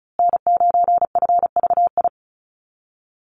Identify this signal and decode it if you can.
Morse code, heard as a keyed tone, about 34 wpm, 700 Hz. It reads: N9FVI
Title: USA Call Signs Repeated 3x 35wpm